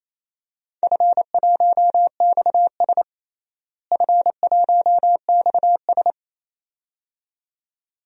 Morse code, heard as a keyed tone, about 27 wpm, 700 Hz. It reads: F1XS F1XS
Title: DXCC-100 Call Signs - 4 Characters - Repeated 2x - 28wpm